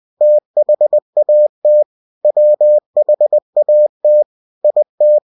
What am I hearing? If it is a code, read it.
Morse code, heard as a keyed tone, about 20 wpm, 600 Hz. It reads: THAT WHAT IT
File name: cw-books / winnie-the-pooh / winnie-20wpm-page-0050